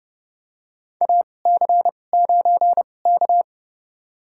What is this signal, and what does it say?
Morse code, heard as a keyed tone, about 30 wpm, spaced as for 22 wpm, 700 Hz. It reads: AC9K
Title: USA Call Signs - 4 Characters - 2x Character Spacing - 30wpm